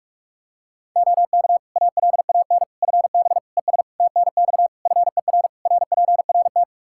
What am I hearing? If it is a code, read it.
Morse code, heard as a keyed tone, about 46 wpm, 700 Hz. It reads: OK ALAN FB ES TNX FER RPRT
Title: Practice QSO - Exchange 2 - Info - 45wpm